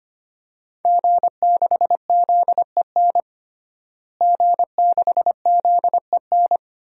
Morse code, heard as a keyed tone, about 25 wpm, 700 Hz. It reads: G6ZEN G6ZEN
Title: DXCC-100 Call Signs - 5 Characters - Repeated 3x - 25wpm